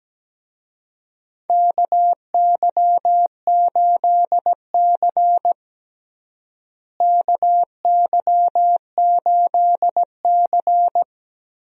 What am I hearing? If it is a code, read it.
Morse code, heard as a keyed tone, about 17 wpm, 700 Hz. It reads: KY8C KY8C
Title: USA Call Signs - 4 Characters - Repeated 3x - 17wpm